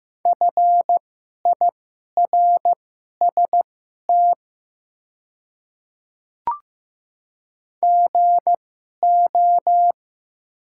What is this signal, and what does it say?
Morse code, heard as a keyed tone, about 15 wpm, spaced as for 11 wpm, 690 Hz. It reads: FIRST GO
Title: 2x Character Spacing - Top 100 Words - 15wpm